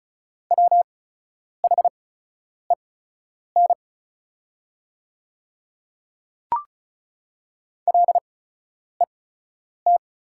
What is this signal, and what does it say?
Morse code, heard as a keyed tone, about 36 wpm, spaced as for 10 wpm, 700 Hz. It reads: WHEN LET
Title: 8x Character Spacing - Top 500 Words - 35wpm